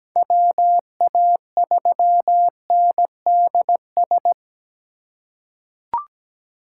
Morse code, heard as a keyed tone, about 17 wpm, 700 Hz. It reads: WA3NDS
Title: USA Call Signs - 6 Characters - 2x Character Spacing - 17wpm